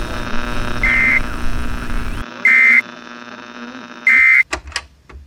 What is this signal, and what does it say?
Emergency Alert System (SAME) tones, heard as harsh buzzing bursts. It is heard from 0.7 to 4.5 s.